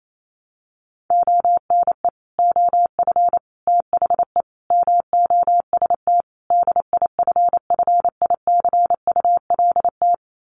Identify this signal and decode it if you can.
Morse code, heard as a keyed tone, about 28 wpm, 700 Hz. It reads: ONE OF THE MOST DIFFICULT
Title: Sets of 5 Words Encore2 28wpm